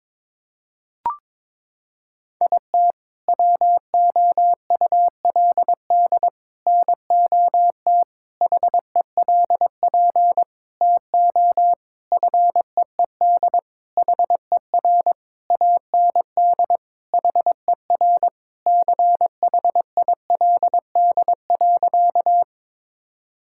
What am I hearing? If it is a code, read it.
Morse code, heard as a keyed tone, about 21 wpm, 700 Hz. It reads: IT WOULD NOT HELP TO FEED HER AND HER CHILD.
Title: Sentences from Top 1600 Words 22wpm